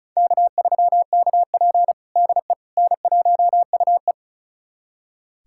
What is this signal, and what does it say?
Morse code, heard as a keyed tone, about 36 wpm, 700 Hz. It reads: K3KP DE N1UE
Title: USA Callsign DE Callsign - 4 Characters - 35wpm